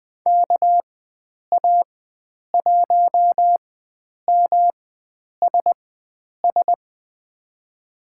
Morse code, heard as a keyed tone, about 20 wpm, spaced as for 9 wpm, 700 Hz. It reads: KA1MSS
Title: USA Call Signs - 6 Characters - 4x Character Spacing - 20wpm